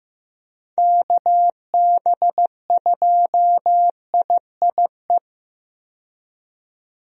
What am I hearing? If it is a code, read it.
Morse code, heard as a keyed tone, about 15 wpm, 700 Hz. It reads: KB2IIE
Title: USA Call Signs - 6 Characters - 15wpm